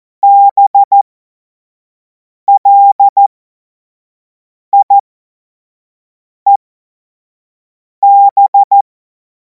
Morse code, heard as a keyed tone, about 14 wpm, spaced as for 5 wpm, 800 Hz. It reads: BLIEB
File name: Morsekurs / Witze / Witz 03. Februar 2026 25 F70